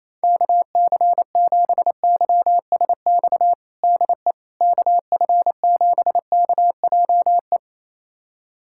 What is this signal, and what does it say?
Morse code, heard as a keyed tone, about 28 wpm, 700 Hz. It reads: KC7YSX DE KF7KJE
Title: USA Callsign DE Callsign - 6 Characters - 28wpm